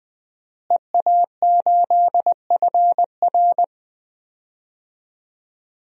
Morse code, heard as a keyed tone, about 20 wpm, 700 Hz. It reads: EA8FR
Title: DXCC-100 Call Signs - 5 Characters - Repeated 3x - 20wpm